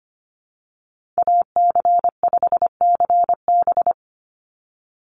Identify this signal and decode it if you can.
Morse code, heard as a keyed tone, about 25 wpm, 700 Hz. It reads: AC5CB
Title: USA Call Signs 25wpm